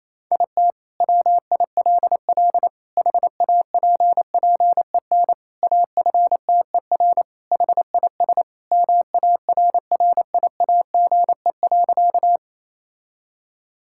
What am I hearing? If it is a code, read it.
Morse code, heard as a keyed tone, about 28 wpm, 700 Hz. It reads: IT WILL HAPPEN AFTER HIS MARRIAGE.
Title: Sentences from Top 1000 Words - Review Part 5 - 28wpm